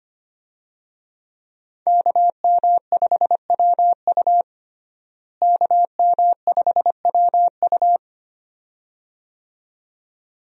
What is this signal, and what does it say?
Morse code, heard as a keyed tone, about 25 wpm, 700 Hz. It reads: KM5WU KM5WU
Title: USA Call Signs - 5 Characters - Repeated 2x - 25wpm